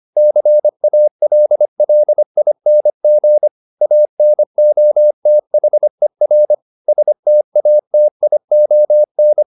Morse code, heard as a keyed tone, about 25 wpm, 600 Hz. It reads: CALLING ANOTHER STATION